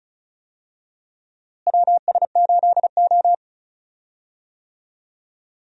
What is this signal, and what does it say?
Morse code, heard as a keyed tone, about 36 wpm, 700 Hz. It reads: WS8O
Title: USA Call Signs - 4 Characters - Repeated 2x - 35wpm